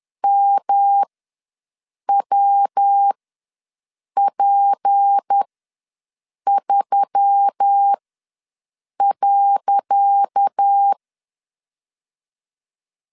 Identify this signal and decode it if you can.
Morse code, heard as a keyed tone, about 10 wpm, spaced as for 6 wpm, 800 Hz. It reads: MWP3.
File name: LEZIONI / 10 SIGNS / 10C DOT ws 10wpm 4213